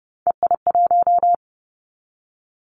Morse code, heard as a keyed tone, about 30 wpm, 700 Hz. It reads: EI1
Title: Callsign Prefixes for Top 50 DXCC - 30wpm